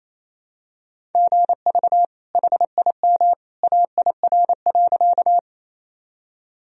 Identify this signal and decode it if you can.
Morse code, heard as a keyed tone, about 28 wpm, 700 Hz. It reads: GV HIM AIR.
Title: Sentences from Top 200 Words and Abbreviations 28wpm